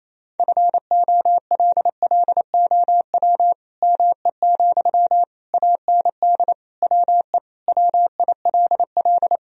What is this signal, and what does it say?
Morse code, heard as a keyed tone, about 28 wpm, 700 Hz. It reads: FOLLOW ME, AND WE WILL
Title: Sentences from Top 1000 Words - Review Part 3 - 28wpm